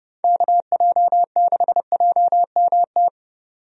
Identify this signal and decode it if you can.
Morse code, heard as a keyed tone, about 31 wpm, 700 Hz. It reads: KJ6JMT
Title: USA Call Signs - 6 Characters - 30wpm